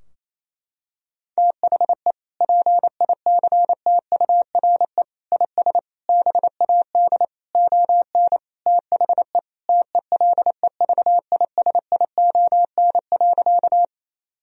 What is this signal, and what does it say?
Morse code, heard as a keyed tone, about 28 wpm, 700 Hz. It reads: THE PICTURE IS BAD ON THE TELEVISION.
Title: Sentences from Top 1000 Words - Review Part 4 - 28wpm